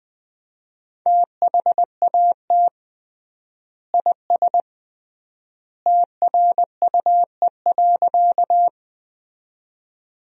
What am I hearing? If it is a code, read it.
Morse code, heard as a keyed tone, about 21 wpm, 700 Hz. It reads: THAT IS TRUE.
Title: 3x Word-Spacing - Sentences from Top 400 Words - 20wpm